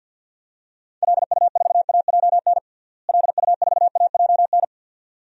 Morse code, heard as a keyed tone, about 48 wpm, 700 Hz. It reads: LU4AJN LU4AJN
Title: DXCC-100 Call Signs - 6 Characters - Repeated 2x - 50wpm